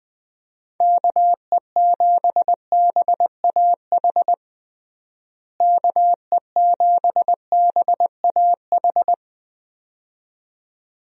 Morse code, heard as a keyed tone, about 20 wpm, 700 Hz. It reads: KE7BAH KE7BAH
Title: USA Call Signs - 6 Characters - Repeated 2x - 20wpm